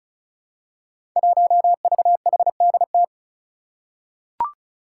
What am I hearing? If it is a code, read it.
Morse code, heard as a keyed tone, about 34 wpm, 700 Hz. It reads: 1VHDT
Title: Five Challenging Letter-Number Combinations 35wpm